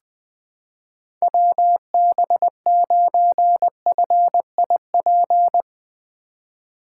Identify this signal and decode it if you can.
Morse code, heard as a keyed tone, about 20 wpm, 700 Hz. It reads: WB9FIP
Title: K1USN SST QSO Practice - 20wpm